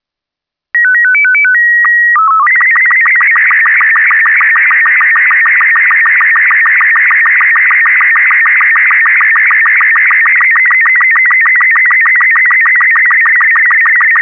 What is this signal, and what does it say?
SSTV (slow-scan television), Robot 36.